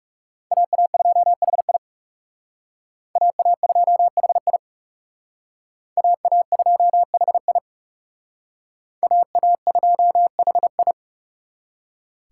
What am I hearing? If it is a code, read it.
Morse code, heard as a keyed tone, about 38 wpm, 700 Hz. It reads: AA2HI AA2HI AA2HI AA2HI
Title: Speed Racing - USA 2x2 Call Signs 30wpm